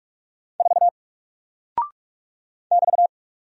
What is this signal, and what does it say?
Morse code, heard as a keyed tone, about 48 wpm, 700 Hz. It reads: 4 <BT>
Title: Single Character (Letter:Number:Punctuation:Prosign) Alt 45wpm